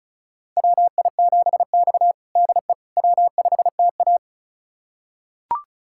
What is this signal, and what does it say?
Morse code, heard as a keyed tone, about 36 wpm, 700 Hz. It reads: WI7X DE W5TA
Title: USA Callsign DE Callsign - 4 Characters - 35wpm